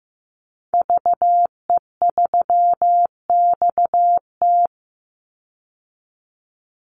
Morse code, heard as a keyed tone, about 15 wpm, 700 Hz. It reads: VE3XT
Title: Call Signs Repeated 2x - 15wpm